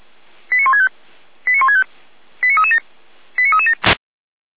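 Five-tone selective call (ZVEI), heard as quick runs of five tones, tones of about 80 ms. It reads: 89165 89165 89337 89337
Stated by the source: ZVEI five-tone selective call received on an SDR, served by feuerwehr-orlamuende (Orlamünde fire brigade).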